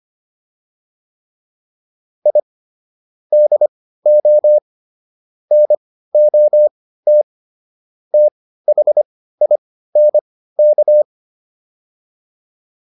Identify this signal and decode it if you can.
Morse code, heard as a keyed tone, about 25 wpm, spaced as for 15 wpm, 600 Hz. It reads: I DO NOT THINK
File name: WD305 15